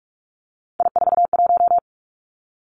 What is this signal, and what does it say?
Morse code, heard as a keyed tone, about 46 wpm, 700 Hz. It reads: I41
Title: Three Challenging Letter-Number Combinations 45wpm